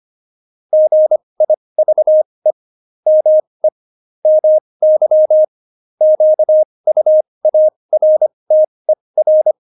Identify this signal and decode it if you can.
Morse code, heard as a keyed tone, about 25 wpm, spaced as for 20 wpm, 620 Hz. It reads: GIVE ME MY QUARTER
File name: Jokes-25-char-20-eff